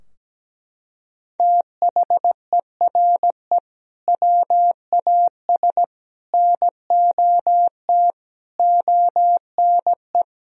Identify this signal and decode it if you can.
Morse code, heard as a keyed tone, about 17 wpm, 700 Hz. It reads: THERE WAS NOT ONE
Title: Sentences from Top 1000 Words - Review Part 2 - 17wpm